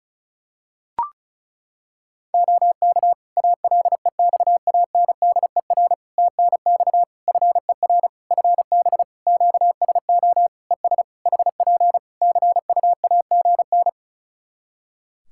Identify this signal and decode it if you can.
Morse code, heard as a keyed tone, about 36 wpm, 700 Hz. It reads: OK ALEXANDER TNX FER FB QSO ES HP CUAGN
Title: K1USN SST QSO Practice 35wpm